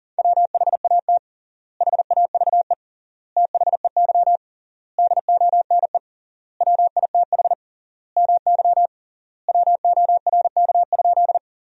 Morse code, heard as a keyed tone, about 40 wpm, 700 Hz. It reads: WHAT HAVE THEY DONE WITH MY WORK?
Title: 3x Word-Spacing - Sentences from Top 500 Words - 40wpm